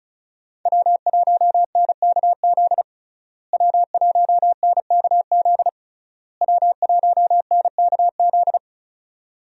Morse code, heard as a keyed tone, about 36 wpm, 700 Hz. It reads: W1NKZ W1NKZ W1NKZ
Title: USA Call Signs - 5 Characters - Repeated 3x - 35wpm